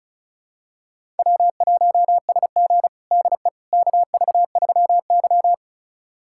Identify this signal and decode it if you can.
Morse code, heard as a keyed tone, about 36 wpm, 700 Hz. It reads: W1SG DE KV3Y
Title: USA Callsign DE Callsign - 4 Characters - 35wpm